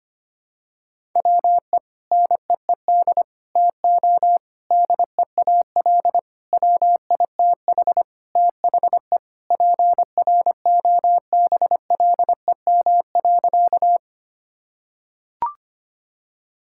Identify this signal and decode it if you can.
Morse code, heard as a keyed tone, about 25 wpm, 700 Hz. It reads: WE NEED TO DEAL WITH THE PROBLEM.
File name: practice / 25wpm / Sentences with 4-Word Phrases - 25wpm